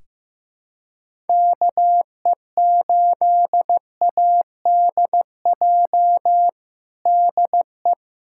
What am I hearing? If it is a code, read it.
Morse code, heard as a keyed tone, about 16 wpm, 700 Hz. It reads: KE8ADJ DE
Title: USA Callsign DE Callsign - 6 Characters - 15wpm